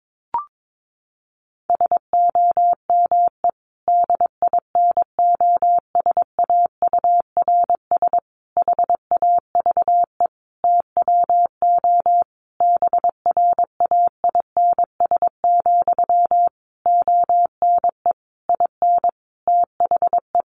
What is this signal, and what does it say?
Morse code, heard as a keyed tone, about 22 wpm, 700 Hz. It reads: SOME DINOSAURS HAVE TWO BRAINS, ONE IN THE
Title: A Sound Of Thunder 22wpm